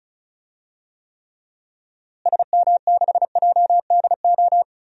Morse code, heard as a keyed tone, about 36 wpm, 700 Hz. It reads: SM6JDO
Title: DXCC-100 Call Signs - 6 Characters - Repeated 3x - 35wpm